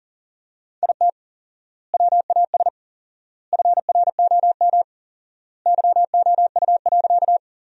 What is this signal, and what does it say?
Morse code, heard as a keyed tone, about 40 wpm, 700 Hz. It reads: IT WAS FROM YOU.
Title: 4x Word-Spacing - Sentences from Top 100 Words - 40wpm